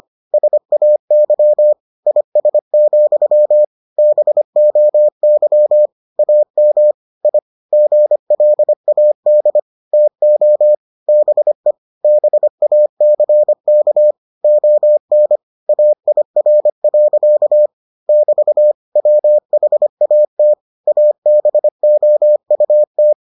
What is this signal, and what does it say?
Morse code, heard as a keyed tone, about 25 wpm, 600 Hz. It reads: SAY IS, BOY AM I GLAD TO BE BACK ON AIR. <BT> WHAT ABOUT